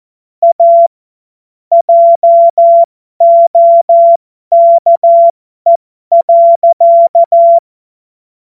Morse code, heard as a keyed tone, about 14 wpm, spaced as for 12 wpm, 680 Hz. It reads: A JOKE.
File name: Morsekurs / Witze